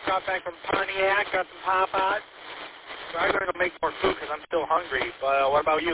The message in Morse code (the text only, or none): none